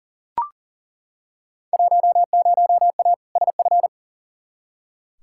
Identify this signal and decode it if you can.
Morse code, heard as a keyed tone, about 40 wpm, 700 Hz. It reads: 10A SF
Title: Field Day Exchanges Encore - 40wpm